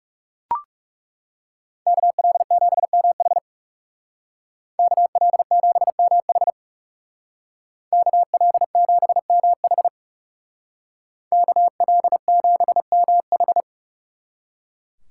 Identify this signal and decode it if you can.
Morse code, heard as a keyed tone, about 36 wpm, 700 Hz. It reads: KL7MH KL7MH KL7MH KL7MH
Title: Speed Racing - USA 2x2 Call Signs 30wpm